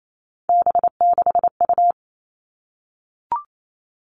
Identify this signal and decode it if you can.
Morse code, heard as a keyed tone, about 28 wpm, 700 Hz. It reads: B6U
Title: Three Challenging Letter-Number Combinations 28wpm